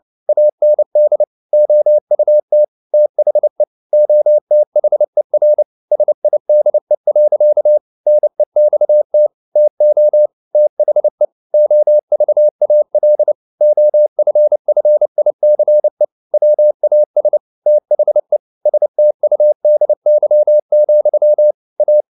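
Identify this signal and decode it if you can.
Morse code, heard as a keyed tone, about 29 wpm, 600 Hz. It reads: AND OUT THE OTHER SIDE. NEXT TO THE OVAL OFFICE WAS THE STUDY, A
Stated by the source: Morse code at 29 wpm, sending Ken Follett's novel Never in English, from CW med Gnister.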